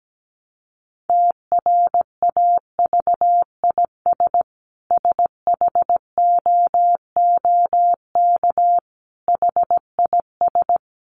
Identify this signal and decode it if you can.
Morse code, heard as a keyed tone, about 17 wpm, 700 Hz. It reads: TRAVIS SHOOK HIS
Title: A Sound Of Thunder 17wpm